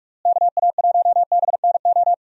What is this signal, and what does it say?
Morse code, heard as a keyed tone, about 46 wpm, 700 Hz. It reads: KA1BNO
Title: USA Call Signs - 6 Characters - Repeated 3x - 45wpm